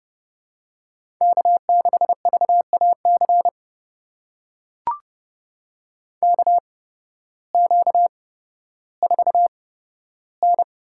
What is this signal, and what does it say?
Morse code, heard as a keyed tone, about 32 wpm, spaced as for 14 wpm, 700 Hz. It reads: K6VAC KQ4N
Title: USA Call Signs - 8x Character Spacing - 30wpm